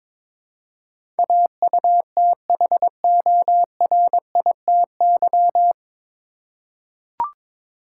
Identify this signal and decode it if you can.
Morse code, heard as a keyed tone, about 22 wpm, 700 Hz. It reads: AUTHORITY
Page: Morse Code Ninja practice 22wpm